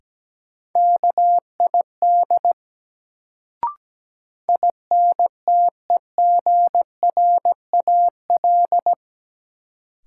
Morse code, heard as a keyed tone, about 17 wpm, 700 Hz. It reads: KID INTEGRAL
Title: Sets of 5 Words - New Word Drill - 17wpm